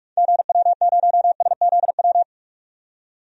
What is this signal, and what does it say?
Morse code, heard as a keyed tone, about 46 wpm, 700 Hz. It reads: GW0SZW
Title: DXCC-100 Call Signs - 6 Characters - 45wpm